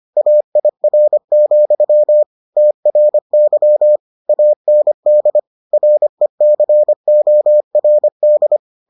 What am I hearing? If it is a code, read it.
Morse code, heard as a keyed tone, about 25 wpm, 600 Hz. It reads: AIR, TRY AND RECORD